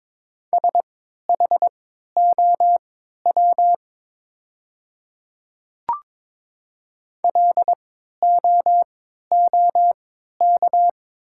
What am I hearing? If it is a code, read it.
Morse code, heard as a keyed tone, about 22 wpm, spaced as for 12 wpm, 700 Hz. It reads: SHOW LOOK
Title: 3x Character Spacing - Top 300 Words - 22wpm